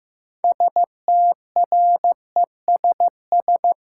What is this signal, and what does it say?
Morse code, heard as a keyed tone, about 15 wpm, 700 Hz. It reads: STRESS
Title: Sentences from Top 1000 Words - Review Part 6 - New Word Drill - 15wpm